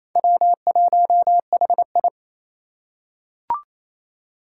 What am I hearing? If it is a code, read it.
Morse code, heard as a keyed tone, about 28 wpm, 700 Hz. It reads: W1HI
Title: Four Challenging Letter-Number Combinations 28wpm